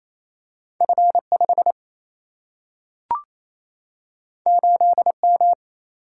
Morse code, heard as a keyed tone, about 28 wpm, 700 Hz. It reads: F 5 8 M
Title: Single-Letter Single-Number 28wpm